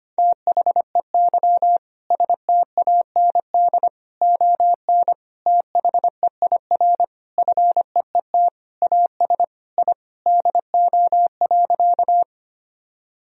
Morse code, heard as a keyed tone, about 25 wpm, 700 Hz. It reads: THEY STAND ON THEIR FEET AS I DO.
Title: Sentences from Top 1000 Words - Review Part 1 - 25wpm